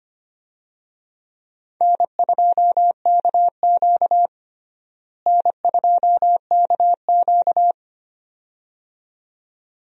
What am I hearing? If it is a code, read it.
Morse code, heard as a keyed tone, about 25 wpm, 700 Hz. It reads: N2KQ N2KQ
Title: USA Call Signs - 4 Characters - Repeated 2x - 25wpm